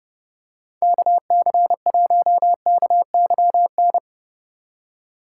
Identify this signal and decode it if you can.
Morse code, heard as a keyed tone, about 30 wpm, 700 Hz. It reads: KC1KYN